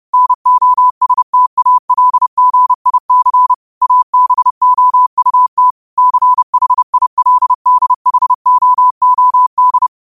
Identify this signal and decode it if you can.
Morse code, heard as a keyed tone, about 30 wpm, 1000 Hz. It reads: NOSTALGIC ABOUT CHILDHOOD
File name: qotdfiles / 30wpm